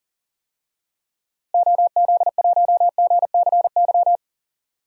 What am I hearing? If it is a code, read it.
Morse code, heard as a keyed tone, about 40 wpm, 700 Hz. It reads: OZ1GCY
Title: DXCC-100 Call Signs - 6 Characters - Repeated 3x - 40wpm